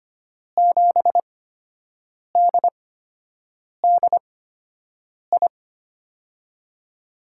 Morse code, heard as a keyed tone, about 25 wpm, spaced as for 7 wpm, 700 Hz. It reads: 7DDI
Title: USA Call Signs - 6 Characters - 8x Character Spacing - 25wpm